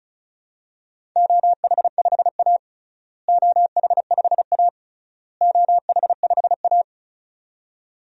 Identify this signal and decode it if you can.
Morse code, heard as a keyed tone, about 36 wpm, 700 Hz. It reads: OH5A OH5A OH5A
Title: DXCC-100 Call Signs - 4 Characters - Repeated 3x - 35wpm